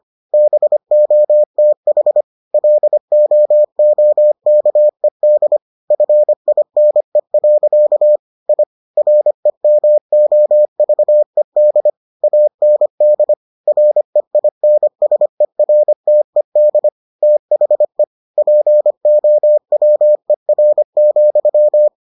Morse code, heard as a keyed tone, about 25 wpm, 600 Hz. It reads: BOTH LOOKED FINE. I REMOVED AND REINSERTED THE POWER,